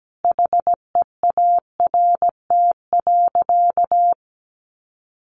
Morse code, heard as a keyed tone, about 17 wpm, 700 Hz. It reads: HEART.
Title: Sentences from Top 1500 Words 17wpm